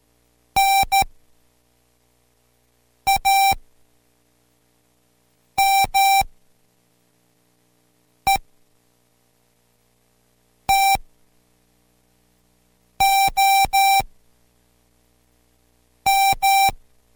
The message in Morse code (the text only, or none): NAME TOM